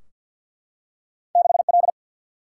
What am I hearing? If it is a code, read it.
Morse code, heard as a keyed tone, about 52 wpm, 700 Hz. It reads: 6L
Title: Single-Letter Single-Number 50wpm